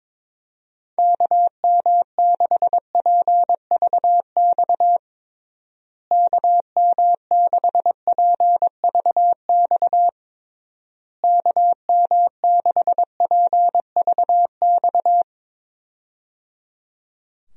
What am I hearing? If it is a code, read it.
Morse code, heard as a keyed tone, about 22 wpm, 700 Hz. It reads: KM6PVX KM6PVX KM6PVX
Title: USA Call Signs - 6 Characters - Repeated 3x - 22wpm